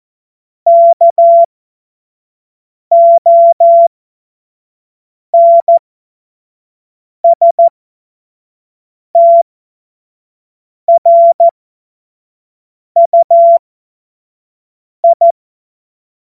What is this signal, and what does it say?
Morse code, heard as a keyed tone, about 14 wpm, spaced as for 5 wpm, 680 Hz. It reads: KONSTRUI
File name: Morsekurs / Witze / Witz 07. Dezember 2021, 25 F70